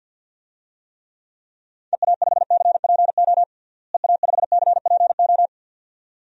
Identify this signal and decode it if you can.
Morse code, heard as a keyed tone, about 51 wpm, 700 Hz. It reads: EA5CPO EA5CPO
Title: DXCC-100 Call Signs - 6 Characters - Repeated 2x - 50wpm